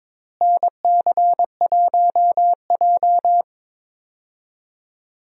Morse code, heard as a keyed tone, about 22 wpm, 700 Hz. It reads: NC1J
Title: USA Call Signs - 4 Characters - Repeated 3x - 22wpm